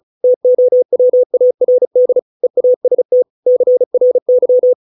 Morse code, heard as a keyed tone, about 34 wpm, 500 Hz. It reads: TOWARD EAST CRY
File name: 632 words / 632 Words 35WPM